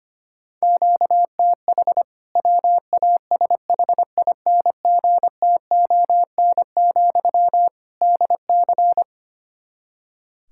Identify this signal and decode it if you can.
Morse code, heard as a keyed tone, about 25 wpm, 700 Hz. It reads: QTH WASHINGTON, DC
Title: K1USN SST QSO Practice 25wpm